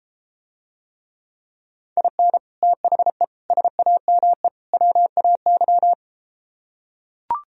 Morse code, heard as a keyed tone, about 33 wpm, 700 Hz. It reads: IN THE SAME WAY